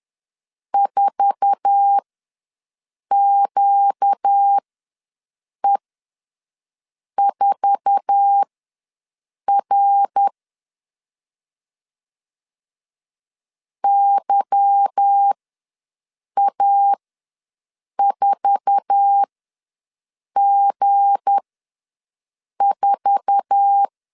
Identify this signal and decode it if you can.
Morse code, heard as a keyed tone, about 10 wpm, spaced as for 6 wpm, 800 Hz. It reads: 4QE4R YA4G4
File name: LEZIONI / 8 YJ45 / 8 YJ4 ws 10wpm 4020